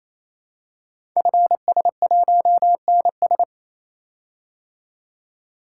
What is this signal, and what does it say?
Morse code, heard as a keyed tone, about 27 wpm, 700 Hz. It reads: FS1NS
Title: Code Groups - 5 Letter:Numbers - 28wpm